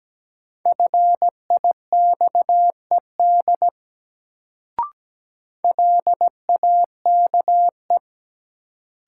Morse code, heard as a keyed tone, about 17 wpm, 700 Hz. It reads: FIXED LAKE
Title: Sentences from Top 1100 Words - New Word Drill Alt - 17wpm